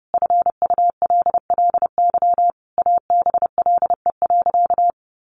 Morse code, heard as a keyed tone, about 30 wpm, 700 Hz. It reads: FULLY ABLE.
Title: Sentences from Top 1300 Words 30wpm